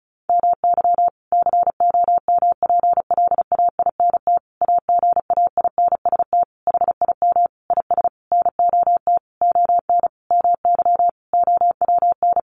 Morse code, heard as a keyed tone, about 34 wpm, 700 Hz. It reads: MY COMPLAINT AGAINST HIM IS NOT ON MY OWN
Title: Sentences from Top 1400 Words 35wpm